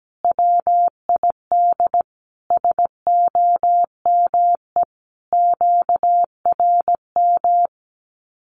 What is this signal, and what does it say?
Morse code, heard as a keyed tone, about 17 wpm, 700 Hz. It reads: WID SOME QRM
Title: Common QSO Phrases 17wpm